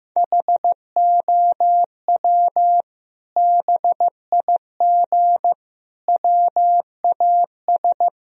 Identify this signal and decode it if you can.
Morse code, heard as a keyed tone, about 15 wpm, 700 Hz. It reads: HOW BIG WAS
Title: Sentences from Top 1000 Words - Review Part 4 - 15wpm